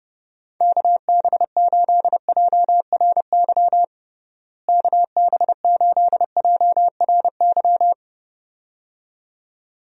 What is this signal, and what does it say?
Morse code, heard as a keyed tone, about 30 wpm, 700 Hz. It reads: KB8JRY KB8JRY
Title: USA Call Signs - 6 Characters - Repeated 2x - 30wpm